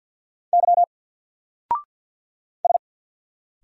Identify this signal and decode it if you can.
Morse code, heard as a keyed tone, about 50 wpm, 700 Hz. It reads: Y S
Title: Single Letter-Number 50wpm